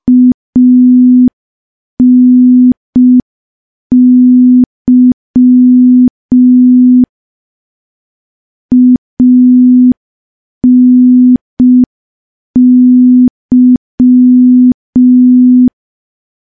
Morse code, heard as a keyed tone, about 5 wpm, 250 Hz. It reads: ANY ANY
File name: morse / morse-training-05WPM